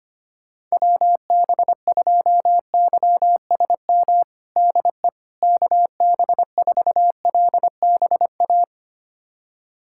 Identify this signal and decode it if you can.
Morse code, heard as a keyed tone, about 25 wpm, 700 Hz. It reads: WB2YSM DE KB4LBA
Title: USA Callsign DE Callsign 25wpm